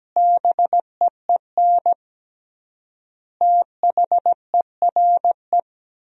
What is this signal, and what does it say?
Morse code, heard as a keyed tone, about 17 wpm, 700 Hz. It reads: BEEN THERE
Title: 3x Word-Spacing - Sentences from Top 400 Words - 17wpm